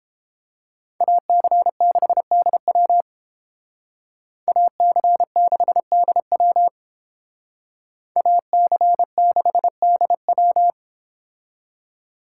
Morse code, heard as a keyed tone, about 31 wpm, 700 Hz. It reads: AC6DW AC6DW AC6DW